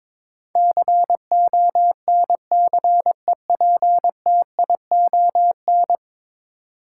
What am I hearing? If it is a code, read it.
Morse code, heard as a keyed tone, about 22 wpm, 700 Hz. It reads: CONCEPTION
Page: morse-code-ninja-practice-22wpm